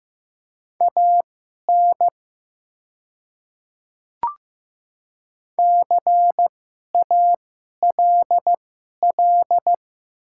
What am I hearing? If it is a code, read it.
Morse code, heard as a keyed tone, about 15 wpm, spaced as for 11 wpm, 690 Hz. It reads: AN CALL
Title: 2x Character Spacing - Top 100 Words - 15wpm